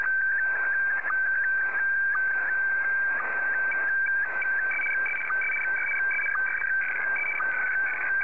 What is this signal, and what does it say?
SSTV (slow-scan television), Scottie DX.
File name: sstv / SDX